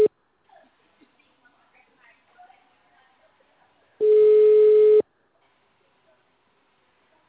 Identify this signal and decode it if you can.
Ringback tone (Europe, 425 Hz), with speech.
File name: out- -2004- - - .14534